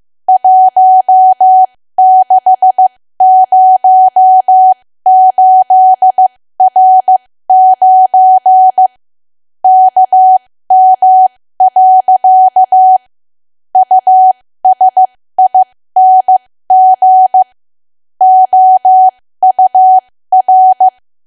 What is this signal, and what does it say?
Morse code, heard as a keyed tone, about 15 wpm, spaced as for 13 wpm, 750 Hz. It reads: 1608R9 KM. USING OUR
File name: cwa / W1AW-Level2 / 13WPM